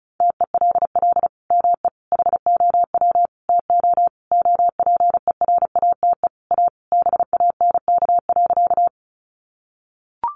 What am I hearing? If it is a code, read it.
Morse code, heard as a keyed tone, about 34 wpm, 700 Hz. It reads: TELL ME HOW TO OPERATE A BANK.
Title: Sentences from Top 1500 Words 35wpm